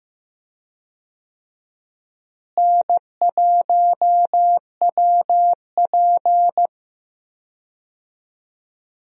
Morse code, heard as a keyed tone, about 15 wpm, 700 Hz. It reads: N1WP
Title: USA Call Signs - 4 Characters - Repeated 3x - 15wpm